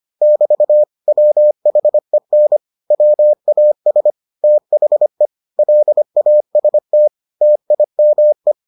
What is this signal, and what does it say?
Morse code, heard as a keyed tone, about 25 wpm, 600 Hz. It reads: <BT> WHEN WAS THE LAST TIME